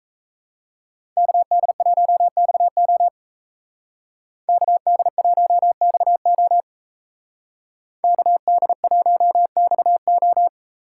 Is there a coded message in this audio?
Yes: Morse code from 1.2 to 10.5 s, heard as a keyed tone, about 36 wpm, 700 Hz. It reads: KD1XO KD1XO KD1XO